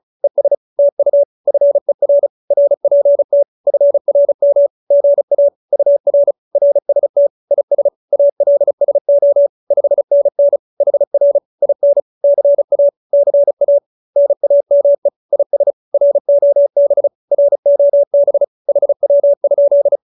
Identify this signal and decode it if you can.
Morse code, heard as a keyed tone, about 34 wpm, 570 Hz. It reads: ES TU FER RPT FRM GA UR RST IS ALSO 5NN HR IN CA CA NAME IS ROB ROB HW?